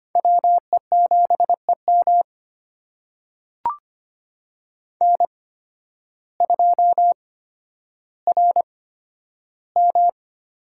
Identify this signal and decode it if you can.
Morse code, heard as a keyed tone, about 25 wpm, spaced as for 12 wpm, 700 Hz. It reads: WE7EM N2RM
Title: USA Call Signs - 8x Character Spacing - 25wpm